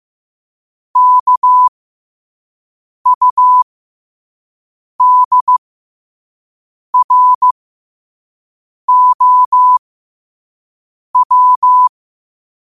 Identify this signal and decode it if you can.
Morse code, heard as a keyed tone, about 15 wpm, spaced as for 5 wpm, 1000 Hz. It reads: KUDROW
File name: qotdfiles / 5wpm